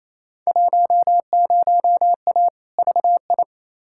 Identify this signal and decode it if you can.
Morse code, heard as a keyed tone, about 28 wpm, 700 Hz. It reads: 10A VI
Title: Field Day Exchanges Encore - 28wpm